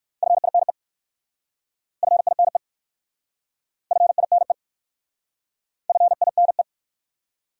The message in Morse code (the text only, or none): FINE FINE FINE FINE